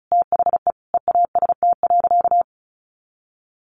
Morse code, heard as a keyed tone, about 34 wpm, 700 Hz. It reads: THE EAST.
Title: Sentences from Top 1300 Words 35wpm